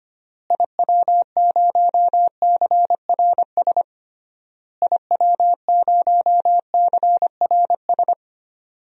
Morse code, heard as a keyed tone, about 25 wpm, 700 Hz. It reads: IW0CRS IW0CRS
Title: DXCC-100 Call Signs - 6 Characters - Repeated 3x - 25wpm